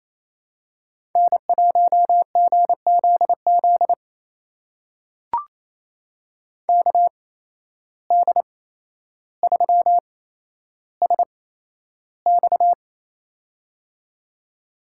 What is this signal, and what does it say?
Morse code, heard as a keyed tone, about 28 wpm, spaced as for 12 wpm, 700 Hz. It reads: N1GZZ KD3SX
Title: USA Call Signs - 5 Characters - 8x Character Spacing - 28wpm